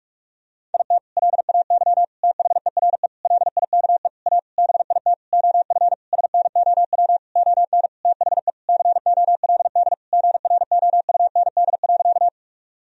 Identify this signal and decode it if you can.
Morse code, heard as a keyed tone, about 41 wpm, 700 Hz. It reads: IT LAY THERE LIKE A BIT OF SNOW ON THE COLD GROUND.
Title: Sentences from Top 1000 Words - Review Part 3 - 45wpm